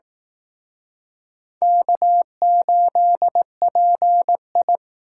Morse code, heard as a keyed tone, about 19 wpm, 700 Hz. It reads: K8PI